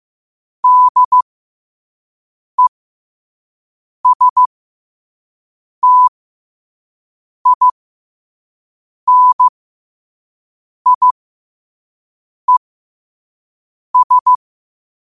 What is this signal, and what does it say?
Morse code, heard as a keyed tone, about 15 wpm, spaced as for 5 wpm, 1000 Hz. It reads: DESTINIES
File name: qotdfiles / 5wpm